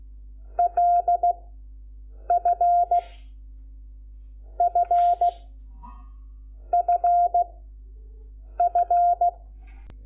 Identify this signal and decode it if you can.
Morse code, heard as a keyed tone, about 16 wpm, spaced as for 6 wpm, 700 Hz. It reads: LFFFF
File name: Lecciones / CLASE Nº5